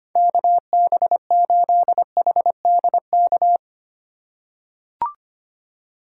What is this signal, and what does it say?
Morse code, heard as a keyed tone, about 25 wpm, 700 Hz. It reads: KB8HDK
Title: USA Call Signs - 6 Characters - 4x Character Spacing - 25wpm